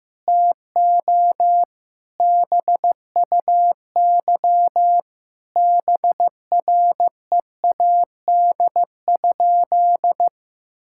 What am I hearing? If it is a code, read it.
Morse code, heard as a keyed tone, about 16 wpm, 700 Hz. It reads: TO BUY BREAD?
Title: Sentences from Top 1600 Words 15wpm